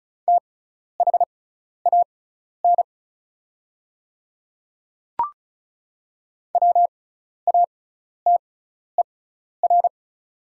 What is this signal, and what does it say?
Morse code, heard as a keyed tone, about 36 wpm, spaced as for 12 wpm, 700 Hz. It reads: THAN WATER
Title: 6x Character Spacing - Top 100 Words - 35wpm